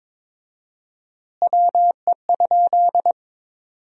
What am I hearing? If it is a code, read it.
Morse code, heard as a keyed tone, about 22 wpm, 700 Hz. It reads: WE?
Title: 4x Word-Spacing - Sentences from Top 200 Words - 22wpm